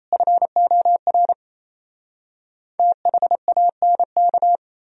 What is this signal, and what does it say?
Morse code, heard as a keyed tone, about 37 wpm, 700 Hz. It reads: FOR THANK